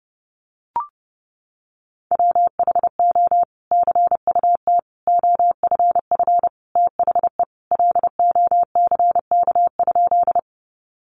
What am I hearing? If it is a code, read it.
Morse code, heard as a keyed tone, about 30 wpm, 700 Hz. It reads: WHO CUT OFF THE LOCK?
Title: Sentences from Top 1300 Words 30wpm